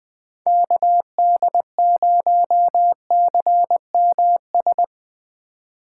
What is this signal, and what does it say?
Morse code, heard as a keyed tone, about 22 wpm, 700 Hz. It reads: KD0CMS
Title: Speed Racing - USA Call Signs 20wpm